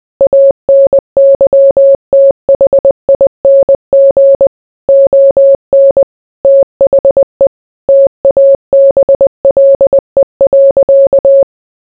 Morse code, heard as a keyed tone, about 20 wpm, 550 Hz. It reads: ANYTHING ON THE TABLE.